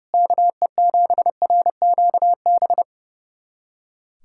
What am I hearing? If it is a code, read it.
Morse code, heard as a keyed tone, about 30 wpm, 700 Hz. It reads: KE7RQB
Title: USA Callsign DE Callsign - 6 Characters - 30wpm